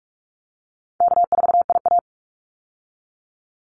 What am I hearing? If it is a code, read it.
Morse code, heard as a keyed tone, about 44 wpm, 700 Hz. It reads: K4IA
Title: Easy Call Signs - 45wpm